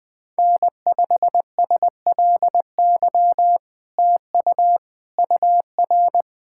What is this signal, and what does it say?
Morse code, heard as a keyed tone, about 20 wpm, 700 Hz. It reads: N5SLY TU UR
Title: POTA QSO Practice - 20wpm